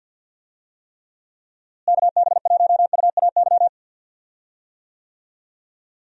Morse code, heard as a keyed tone, about 50 wpm, 700 Hz. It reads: KB1UAY
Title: USA Call Signs - 6 Characters - Repeated 2x - 50wpm